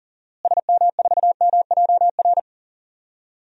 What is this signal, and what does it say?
Morse code, heard as a keyed tone, about 40 wpm, 700 Hz. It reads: SM4MJR
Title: DXCC-100 Call Signs - 6 Characters - Repeated 2x - 40wpm